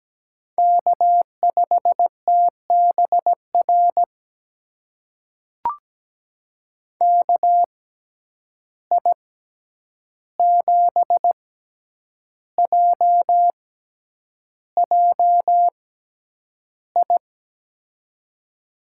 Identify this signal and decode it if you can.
Morse code, heard as a keyed tone, about 17 wpm, spaced as for 8 wpm, 700 Hz. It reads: K5TBR KI7JJI
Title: USA Call Signs - 6x Character Spacing - 17wpm